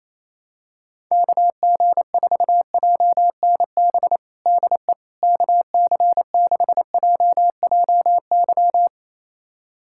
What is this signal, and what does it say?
Morse code, heard as a keyed tone, about 28 wpm, 700 Hz. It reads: KG4JNB DE KC6JJY
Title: USA Callsign DE Callsign 28wpm